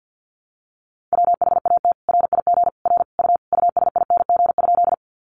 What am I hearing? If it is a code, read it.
Morse code, heard as a keyed tone, about 50 wpm, 700 Hz. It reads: WHAT RIG R U USING?
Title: Common QSO Phrases 50wpm